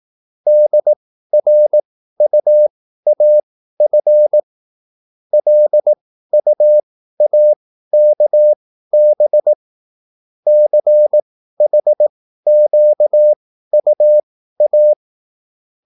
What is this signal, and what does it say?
Morse code, heard as a keyed tone, about 18 wpm, spaced as for 13 wpm, 600 Hz. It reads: DRUAF LUAKB CHQUA